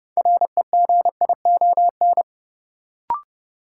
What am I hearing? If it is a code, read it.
Morse code, heard as a keyed tone, about 32 wpm, 700 Hz. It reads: REGION
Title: Sentences from Top 1500 Words - New Word Drill - 30wpm